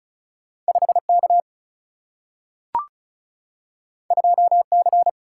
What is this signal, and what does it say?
Morse code, heard as a keyed tone, about 36 wpm, 700 Hz. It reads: 5 K 2 C
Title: Single-Letter Single-Number 35wpm